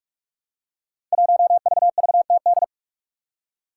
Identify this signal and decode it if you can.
Morse code, heard as a keyed tone, about 44 wpm, 700 Hz. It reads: 1VVTD
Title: Five Challenging Letter-Number Combinations 45wpm